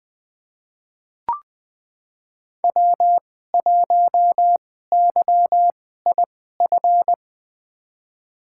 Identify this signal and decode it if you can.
Morse code, heard as a keyed tone, about 20 wpm, spaced as for 14 wpm, 700 Hz. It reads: W1YIF
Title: USA Call Signs - 5 Characters - 2x Character Spacing - 20wpm